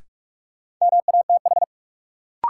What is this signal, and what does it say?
Morse code, heard as a keyed tone, about 45 wpm, 700 Hz. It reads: MATH
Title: Sentences from Top 1000 Words - Review Part 6 - New Word Drill - 45wpm